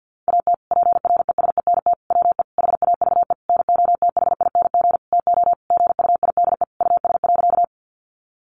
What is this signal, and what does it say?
Morse code, heard as a keyed tone, about 50 wpm, 700 Hz. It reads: AT PRESENT WE HAVE NOTHING TO GUIDE US.
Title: Sentences from Top 800 Words 50wpm